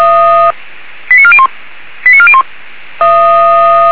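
Five-tone selective call (ZVEI), heard as quick runs of five tones, tones of about 70 ms. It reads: 89401 89401
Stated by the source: ZVEI five-tone call received by SDR, audio.f85360.feuerwehr-orlamuende.org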